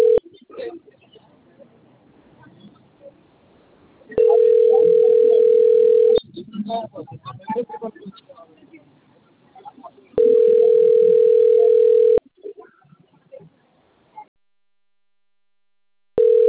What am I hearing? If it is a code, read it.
Ringback tone (North America), 440 + 480 Hz, with speech.
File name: out- - - - - .50382